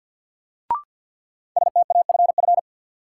Morse code, heard as a keyed tone, about 51 wpm, 700 Hz. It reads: STAFF
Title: Sentences from Top 1200 Words - New Word Drill - 50wpm